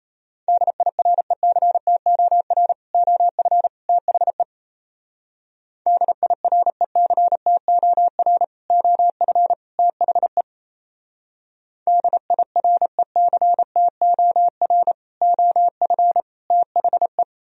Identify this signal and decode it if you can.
Morse code, heard as a keyed tone, about 36 wpm, 700 Hz. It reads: DIRECTOR OF THE DIRECTOR OF THE DIRECTOR OF THE